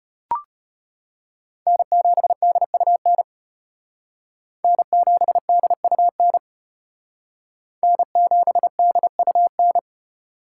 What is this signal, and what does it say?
Morse code, heard as a keyed tone, about 34 wpm, 700 Hz. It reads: N7DUN N7DUN N7DUN